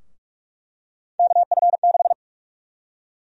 Morse code, heard as a keyed tone, about 46 wpm, 700 Hz. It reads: KF6
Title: USA Call Sign Prefixes - 45wpm